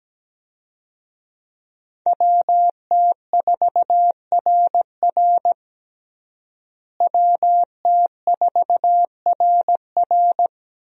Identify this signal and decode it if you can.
Morse code, heard as a keyed tone, about 17 wpm, 700 Hz. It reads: WT4RR WT4RR
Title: USA Call Signs - 5 Characters - Repeated 2x - 17wpm